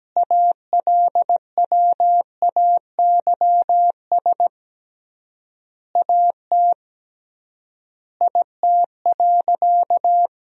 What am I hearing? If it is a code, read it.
Morse code, heard as a keyed tone, about 17 wpm, 700 Hz. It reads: ALWAYS AT IT.
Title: 3x Word-Spacing - Sentences from Top 300 Words - 17wpm